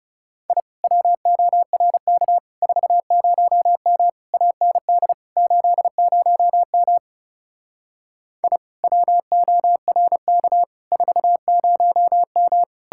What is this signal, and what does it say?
Morse code, heard as a keyed tone, about 38 wpm, 700 Hz. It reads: I WORK 40M AND 80M I WORK 40M